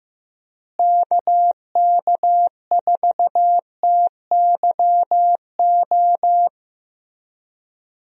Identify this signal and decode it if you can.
Morse code, heard as a keyed tone, about 15 wpm, 700 Hz. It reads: KK4TYO
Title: USA Call Signs - 6 Characters - Repeated 3x - 15wpm